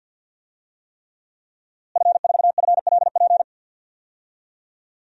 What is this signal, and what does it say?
Morse code, heard as a keyed tone, about 53 wpm, 700 Hz. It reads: F4FLP
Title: DXCC-100 Call Signs - 5 Characters - Repeated 3x - 50wpm